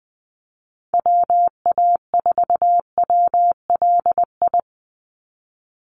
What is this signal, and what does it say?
Morse code, heard as a keyed tone, about 19 wpm, 700 Hz. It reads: WA4WLI
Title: Call Signs - 20wpm